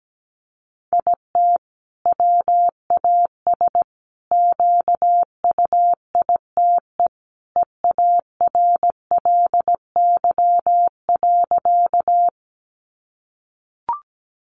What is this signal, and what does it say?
Morse code, heard as a keyed tone, about 17 wpm, 700 Hz. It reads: IT WAS QUITE EARLY.
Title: Sentences from Top 700 Words 17wpm